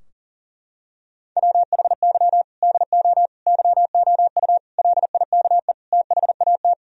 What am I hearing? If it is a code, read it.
Morse code, heard as a keyed tone, about 40 wpm, 700 Hz. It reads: WHY DO YOU LIKE THAT
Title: Sentences from Top 1000 Words - Review Part 4 - 40wpm